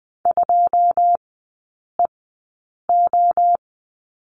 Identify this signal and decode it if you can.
Morse code, heard as a keyed tone, about 20 wpm, spaced as for 9 wpm, 690 Hz. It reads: 2EO
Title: 081 Lesson 14e - Number 2 - Three Characters 20wpm